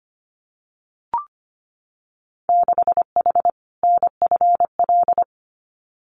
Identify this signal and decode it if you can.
Morse code, heard as a keyed tone, about 25 wpm, 700 Hz. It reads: 6H NFL
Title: Winter Field Day Exchanges Encore 25wpm